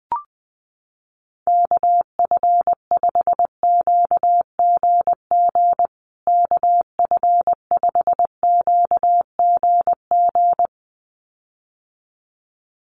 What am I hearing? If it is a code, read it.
Morse code, heard as a keyed tone, about 20 wpm, 700 Hz. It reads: KF5QGG KF5QGG
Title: USA Call Signs Repeated 2x 20wpm